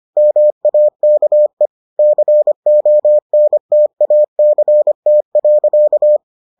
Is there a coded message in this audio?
Yes: Morse code from 0.2 to 6.2 s, heard as a keyed tone, about 25 wpm, 600 Hz. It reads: MAKE CONTACT.